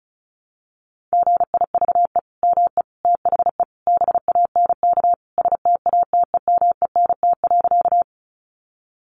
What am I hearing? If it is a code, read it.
Morse code, heard as a keyed tone, about 34 wpm, 700 Hz. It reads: GIVE ME THE BANK STATEMENT.
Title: Sentences from Top 1200 Words 35wpm